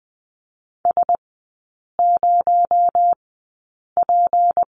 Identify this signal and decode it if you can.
Morse code, heard as a keyed tone, about 20 wpm, spaced as for 8 wpm, 700 Hz. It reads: S0P
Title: 218 Lesson 35e - Number 0 - Three Characters 20wpm